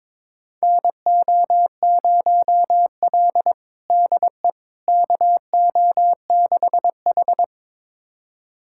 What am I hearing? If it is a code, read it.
Morse code, heard as a keyed tone, about 22 wpm, 700 Hz. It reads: NO0L DE KO6H
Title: USA Callsign DE Callsign - 4 Characters - 22wpm